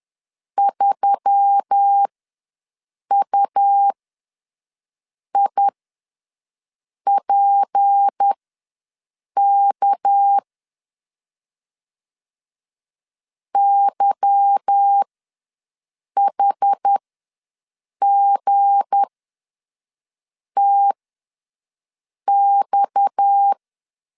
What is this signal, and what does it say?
Morse code, heard as a keyed tone, about 10 wpm, spaced as for 5 wpm, 800 Hz. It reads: 3UIPK YHGTX